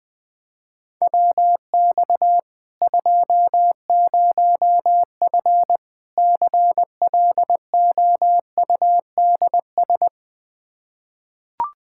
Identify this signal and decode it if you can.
Morse code, heard as a keyed tone, about 20 wpm, 700 Hz. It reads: WX 20F CLOUDS
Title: Practice QSO - Exchange 2 - Info - 20wpm